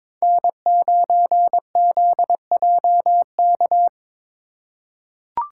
Morse code, heard as a keyed tone, about 22 wpm, 700 Hz. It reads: N9ZJK
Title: USA Call Signs - 5 Characters - 4x Character Spacing - 22wpm